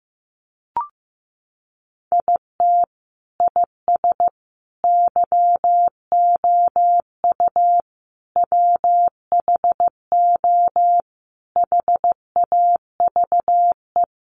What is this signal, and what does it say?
Morse code, heard as a keyed tone, about 15 wpm, 700 Hz. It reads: IT IS YOU WHO HAVE
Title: Sentences from Top 1200 Words 15wpm